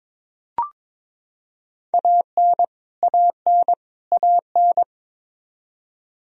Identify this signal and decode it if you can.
Morse code, heard as a keyed tone, about 21 wpm, 700 Hz. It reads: AN AN AN
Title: Top 100 Words Repeated 22wpm